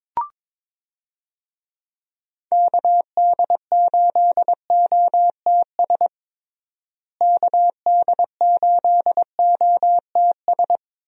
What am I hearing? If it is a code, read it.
Morse code, heard as a keyed tone, about 22 wpm, 700 Hz. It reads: KD8OTS KD8OTS
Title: USA Call Signs - 6 Characters - Repeated 3x - 22wpm